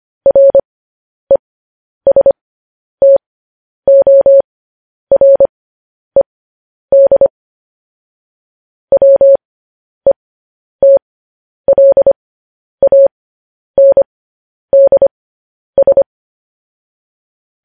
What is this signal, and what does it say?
Morse code, heard as a keyed tone, about 25 wpm, spaced as for 10 wpm, 550 Hz. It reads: RESTORED WETLANDS
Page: News Headlines in Morse Code at 10 WPM